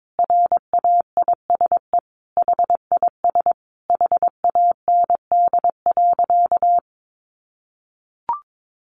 Morse code, heard as a keyed tone, about 22 wpm, 700 Hz. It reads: RAISE HIS HAND.
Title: Sentences from Top 1400 Words 22wpm